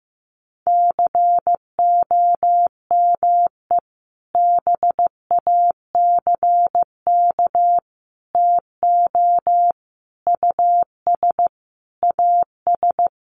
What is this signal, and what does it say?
Morse code, heard as a keyed tone, about 15 wpm, 700 Hz. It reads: COME BACK TO US AS
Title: Sentences from Top 500 Words 15wpm